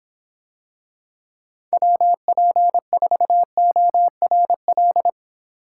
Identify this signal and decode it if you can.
Morse code, heard as a keyed tone, about 26 wpm, 700 Hz. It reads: WP4ORL